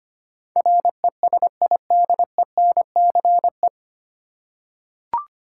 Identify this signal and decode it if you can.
Morse code, heard as a keyed tone, about 25 wpm, 700 Hz. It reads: RESIDENCE
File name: Nine Letter Words - 25wpm